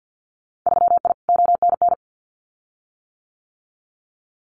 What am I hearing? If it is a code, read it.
Morse code, heard as a keyed tone, about 50 wpm, 700 Hz. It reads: 3I ONN
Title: Winter Field Day Exchanges Encore 50wpm